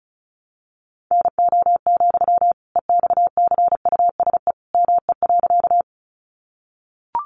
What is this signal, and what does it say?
Morse code, heard as a keyed tone, about 34 wpm, 700 Hz. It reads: NO, EXCUSE ME.
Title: Sentences from Top 700 Words 35wpm